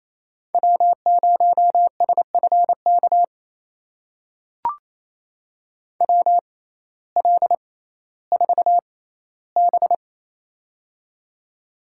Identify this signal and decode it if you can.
Morse code, heard as a keyed tone, about 28 wpm, spaced as for 15 wpm, 700 Hz. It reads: W0SFK WL4B
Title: USA Call Signs - 6x Character Spacing - 28wpm